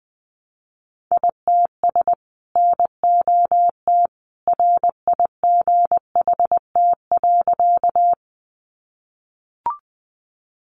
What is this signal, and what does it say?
Morse code, heard as a keyed tone, about 20 wpm, 700 Hz. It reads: ITS NOT RIGHT.
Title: Sentences from Top 600 Words 20wpm